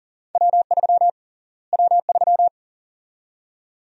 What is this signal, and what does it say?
Morse code, heard as a keyed tone, about 40 wpm, 700 Hz. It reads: W3 W3
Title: USA Call Sign Prefixes - Repeated 2x - 40wpm